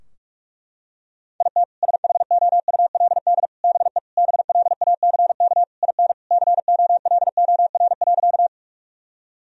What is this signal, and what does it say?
Morse code, heard as a keyed tone, about 44 wpm, 700 Hz. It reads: IT SHOULD BE BLACK IN COLOR.
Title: Sentences from Top 1000 Words - Review Part 2 - 45wpm